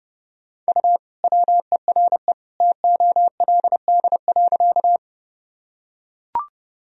Morse code, heard as a keyed tone, about 30 wpm, 700 Hz. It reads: U WERE TOLD.
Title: Sentences from Top 400 Words and Abbreviations 30wpm